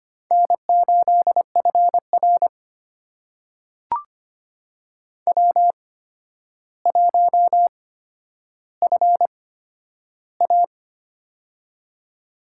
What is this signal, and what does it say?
Morse code, heard as a keyed tone, about 25 wpm, spaced as for 11 wpm, 700 Hz. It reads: N8FR W1FA
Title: USA Call Signs - 4 Characters - 8x Character Spacing - 25wpm